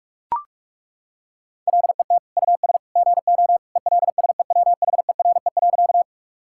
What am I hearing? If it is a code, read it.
Morse code, heard as a keyed tone, about 41 wpm, 700 Hz. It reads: LET US GO ELSEWHERE.
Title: Sentences from Top 1600 Words 45wpm